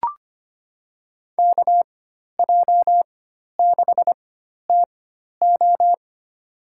Morse code, heard as a keyed tone, about 25 wpm, spaced as for 12 wpm, 700 Hz. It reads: KJ6TO
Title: USA Call Signs - 5 Characters - 4x Character Spacing - 25wpm